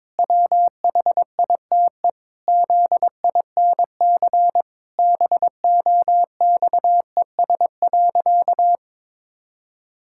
Morse code, heard as a keyed tone, about 22 wpm, 700 Hz. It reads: WHITE ZINC BOXES.
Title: Pangram Challenge v2 - 22wpm